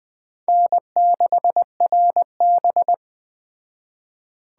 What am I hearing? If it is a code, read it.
Morse code, heard as a keyed tone, about 20 wpm, 700 Hz. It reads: N6RB